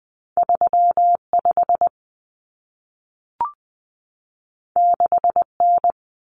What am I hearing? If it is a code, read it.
Morse code, heard as a keyed tone, about 20 wpm, 700 Hz. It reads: 35 6N
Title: Common Cut Numbers - 2-Digit 20wpm